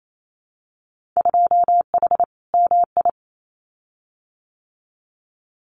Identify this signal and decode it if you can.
Morse code, heard as a keyed tone, about 28 wpm, 700 Hz. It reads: 2H MI
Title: Winter Field Day Exchanges 28wpm